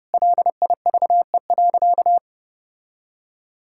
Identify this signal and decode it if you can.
Morse code, heard as a keyed tone, about 30 wpm, 700 Hz. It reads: LIVE.
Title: Sentences from Top 1000 Words - Review Part 5 - 30wpm